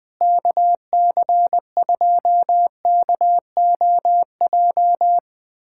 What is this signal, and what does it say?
Morse code, heard as a keyed tone, about 20 wpm, 700 Hz. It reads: KC2KOJ
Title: USA Callsign DE Callsign 20wpm